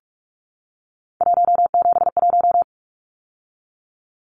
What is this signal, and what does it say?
Morse code, heard as a keyed tone, about 44 wpm, 700 Hz. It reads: 171
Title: Numbers - 3-Digit 45wpm